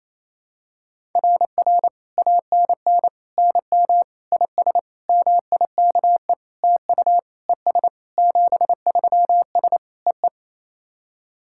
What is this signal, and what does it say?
Morse code, heard as a keyed tone, about 28 wpm, 700 Hz. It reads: RR ANN NM IS MIKE TU ES 73S EE
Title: SOTA QSO Practice 28wpm